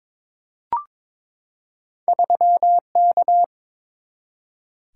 Morse code, heard as a keyed tone, about 22 wpm, 690 Hz. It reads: 3K
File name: practice / 22wpm / Two Letter-Number Combinations - 22wpm